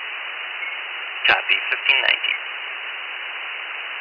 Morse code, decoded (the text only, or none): none